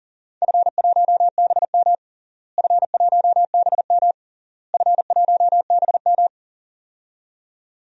Morse code, heard as a keyed tone, about 40 wpm, 700 Hz. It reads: F1BM F1BM F1BM
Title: DXCC-100 Call Signs - 4 Characters - Repeated 3x - 40wpm